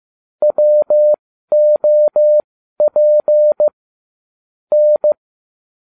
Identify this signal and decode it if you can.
Morse code, heard as a keyed tone, about 15 wpm, spaced as for 12 wpm, 600 Hz. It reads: WOP N